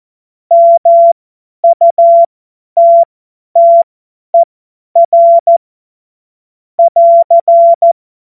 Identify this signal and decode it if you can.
Morse code, heard as a keyed tone, about 14 wpm, spaced as for 10 wpm, 680 Hz. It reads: MUTTER <AR>
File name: Morsekurs / Witze / Witz 07. Dezember 2021, 50 F70